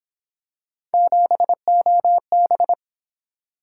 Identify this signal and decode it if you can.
Morse code, heard as a keyed tone, about 26 wpm, 700 Hz. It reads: 7OB
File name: Speed-Racing - Three Letter-Number Combinations - 22wpm